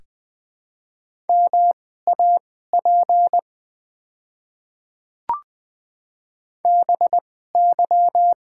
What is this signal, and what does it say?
Morse code, heard as a keyed tone, about 20 wpm, spaced as for 14 wpm, 690 Hz. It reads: MAP BY
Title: 2x Character Spacing - Top 500 Words - 20wpm